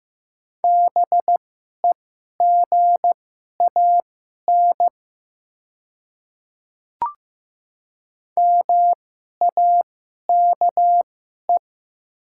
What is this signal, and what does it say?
Morse code, heard as a keyed tone, about 15 wpm, spaced as for 11 wpm, 690 Hz. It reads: BEGAN MAKE
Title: 2x Character Spacing - Top 500 Words - 15wpm